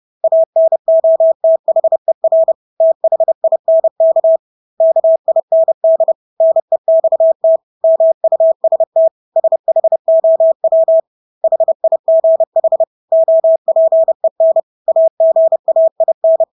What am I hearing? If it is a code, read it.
Morse code, heard as a keyed tone, about 30 wpm, 650 Hz. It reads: ANOTHER THINK KIND NEXT MUST SHOW HIGH OPEN AGAIN
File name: Engelske ord 30-30 wpm